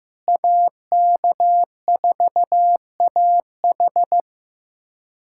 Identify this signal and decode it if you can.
Morse code, heard as a keyed tone, about 15 wpm, 700 Hz. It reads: AK4AH
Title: USA Callsign DE Callsign 15wpm